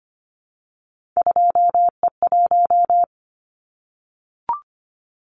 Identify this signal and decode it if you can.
Morse code, heard as a keyed tone, about 25 wpm, 700 Hz. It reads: 2E1
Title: Callsign Prefixes for Top 25 DXCC - Repeated 3x - 25wpm